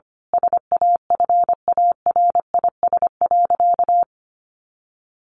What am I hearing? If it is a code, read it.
Morse code, heard as a keyed tone, about 25 wpm, 700 Hz. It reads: SAFARIS.
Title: A Sound Of Thunder 25wpm